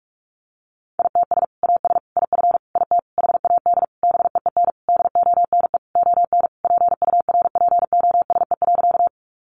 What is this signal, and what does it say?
Morse code, heard as a keyed tone, about 44 wpm, 700 Hz. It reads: ITS AS IF IT HAD BEEN DONE ON PURPOSE.
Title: Sentences from Top 800 Words 45wpm